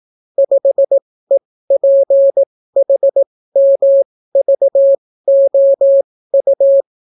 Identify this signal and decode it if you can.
Morse code, heard as a keyed tone, about 18 wpm, spaced as for 14 wpm, 550 Hz. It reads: 5EPHMVOU